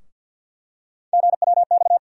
Morse code, heard as a keyed tone, about 53 wpm, 700 Hz. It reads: GWX
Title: USA Call Sign Postfixes - 3 Letters 50wpm